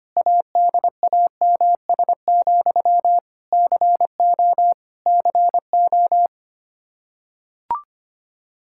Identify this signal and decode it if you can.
Morse code, heard as a keyed tone, about 25 wpm, 700 Hz. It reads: ADAMS, CO CO
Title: Practice QSO - Exchange 1 - Beginning - 25wpm